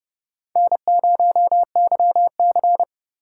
Morse code, heard as a keyed tone, about 28 wpm, 700 Hz. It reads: N0YC